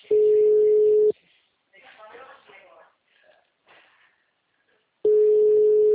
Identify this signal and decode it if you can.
Ringback tone (Europe, 425 Hz), with speech.